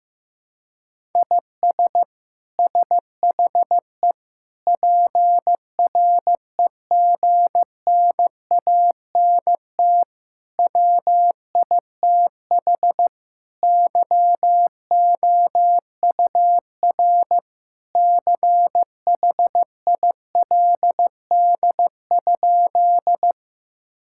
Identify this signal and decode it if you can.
Morse code, heard as a keyed tone, about 16 wpm, 700 Hz. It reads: IS SHE PREGNANT WITH YOUR CHILD?
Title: Sentences from Top 1000 Words - Review Part 6 - 15wpm